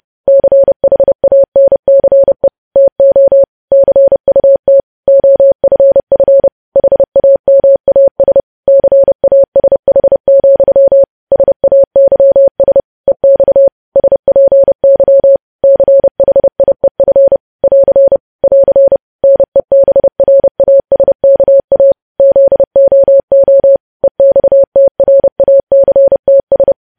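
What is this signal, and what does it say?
Morse code, heard as a keyed tone, about 30 wpm, 550 Hz. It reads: CHANCE TO CUT OFF HAMAS CASH, SAYS EX SPY CHIEF <AR> <AR> NEBRASKA ZOO EXTRACTS
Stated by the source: Morse news headlines sent at 30.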